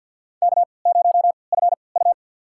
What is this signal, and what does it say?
Morse code, heard as a keyed tone, about 53 wpm, spaced as for 29 wpm, 700 Hz. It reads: K0FU
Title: USA Call Signs - 4 Characters - 3x Character Spacing - 50wpm